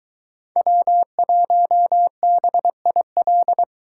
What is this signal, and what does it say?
Morse code, heard as a keyed tone, about 21 wpm, 700 Hz. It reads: W1BIL